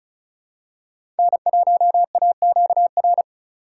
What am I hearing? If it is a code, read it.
Morse code, heard as a keyed tone, about 36 wpm, 700 Hz. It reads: N1AQR
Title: USA Call Signs - 5 Characters - 4x Character Spacing - 35wpm